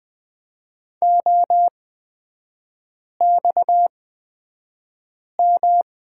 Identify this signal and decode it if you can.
Morse code, heard as a keyed tone, about 20 wpm, spaced as for 5 wpm, 690 Hz. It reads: OXM
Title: 255 Lesson 41e - Period - Three Characters 20wpm